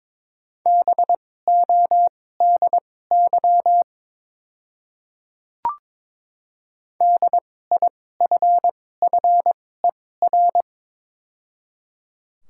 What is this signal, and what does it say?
Morse code, heard as a keyed tone, about 22 wpm, spaced as for 16 wpm, 700 Hz. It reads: BODY DIFFER
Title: 2x Character Spacing - Top 500 Words - 22wpm